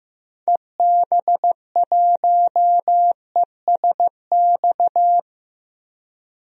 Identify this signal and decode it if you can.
Morse code, heard as a keyed tone, about 15 wpm, 700 Hz. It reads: EB1ESX
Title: DXCC-100 Call Signs - 6 Characters - 15wpm